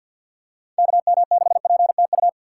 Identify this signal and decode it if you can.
Morse code, heard as a keyed tone, about 53 wpm, 700 Hz. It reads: KM6PTU
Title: USA Call Signs - 6x Character Spacing - 50wpm